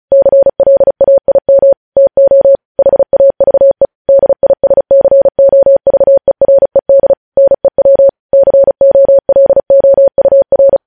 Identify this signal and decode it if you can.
Morse code, heard as a keyed tone, about 36 wpm, 550 Hz. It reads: CLAIM TO HAVE DISCOVERED NEW COLOUR
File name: podcast / News headlines-1x-35x35